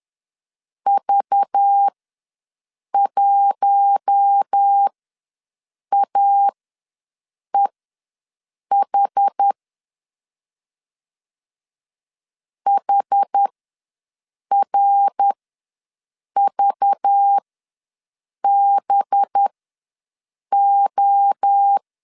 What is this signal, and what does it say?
Morse code, heard as a keyed tone, about 10 wpm, spaced as for 6 wpm, 800 Hz. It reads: V1AEH HRVBO
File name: LEZIONI / 5 LBVC / 5 LBV ws 10wpm 3645